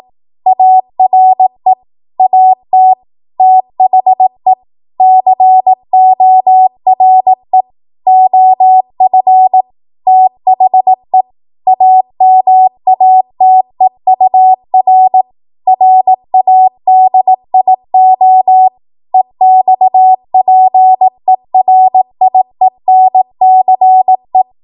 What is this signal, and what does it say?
Morse code, heard as a keyed tone, about 18 wpm, 750 Hz. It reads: ARE AT THE CORE OF THE AMATEUR RADIO EXPERIENCE
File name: Morse / 18WPM